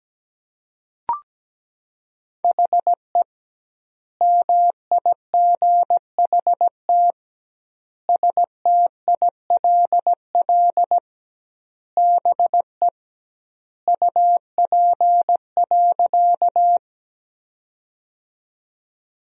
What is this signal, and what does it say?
Morse code, heard as a keyed tone, about 17 wpm, 700 Hz. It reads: HE MIGHT STILL BE UP.
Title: 2x Word-Spacing - Sentences from Top 300 Words - 17wpm